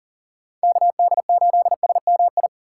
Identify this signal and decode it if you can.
Morse code, heard as a keyed tone, about 40 wpm, 700 Hz. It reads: KD8SMI
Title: USA Call Signs - 3x Character Spacing - 40wpm